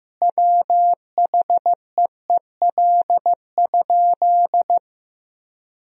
Morse code, heard as a keyed tone, about 16 wpm, 700 Hz. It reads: WHEEL?
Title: Sentences from Top 1000 Words - Review Part 3 - 15wpm